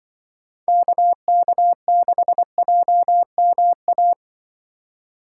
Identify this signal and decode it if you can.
Morse code, heard as a keyed tone, about 25 wpm, 700 Hz. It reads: KK6JMA
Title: Speed Racing - USA 2x3 Call Signs 20wpm